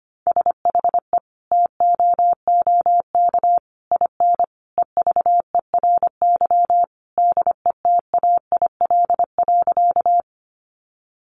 Morse code, heard as a keyed tone, about 25 wpm, 700 Hz. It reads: SHE TOOK IN EVERY DETAIL.
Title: Sentences from Top 1500 Words 25wpm